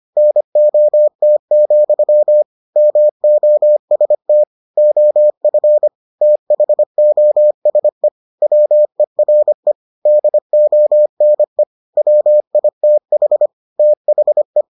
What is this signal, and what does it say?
Morse code, heard as a keyed tone, about 25 wpm, 600 Hz. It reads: NOT, MOST OF THOSE WERE DONE WITH THE